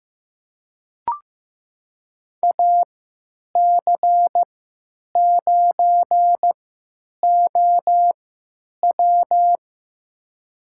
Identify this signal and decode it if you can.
Morse code, heard as a keyed tone, about 15 wpm, spaced as for 9 wpm, 700 Hz. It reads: AC9OW
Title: USA Call Signs - 5 Characters - 3x Character Spacing - 15wpm